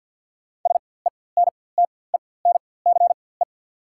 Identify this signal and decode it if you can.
Morse code, heard as a keyed tone, about 48 wpm, spaced as for 23 wpm, 690 Hz. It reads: SENTENCE
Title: 4x Character Spacing - Top 300 Words - 50wpm